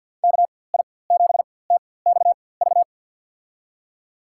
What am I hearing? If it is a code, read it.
Morse code, heard as a keyed tone, about 52 wpm, spaced as for 23 wpm, 700 Hz. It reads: KI7TXV
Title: USA Call Signs - 4x Character Spacing - 50wpm